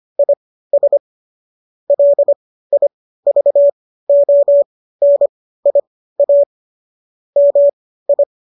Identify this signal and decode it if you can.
Morse code, heard as a keyed tone, about 25 wpm, spaced as for 15 wpm, 570 Hz. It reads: IS LIVONIA MI